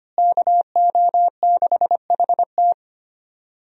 Morse code, heard as a keyed tone, about 25 wpm, 700 Hz. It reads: KO6HT
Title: USA Call Signs - 5 Characters - Repeated 2x - 25wpm